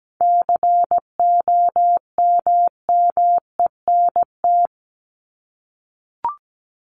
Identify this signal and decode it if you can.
Morse code, heard as a keyed tone, about 17 wpm, 700 Hz. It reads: COMMENT
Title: Top 1751-2000 Words 17wpm